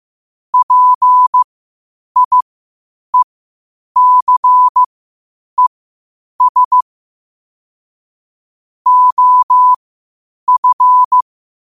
Morse code, heard as a keyed tone, about 15 wpm, spaced as for 8 wpm, 1000 Hz. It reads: PIECES OF
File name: qotdfiles / 8wpm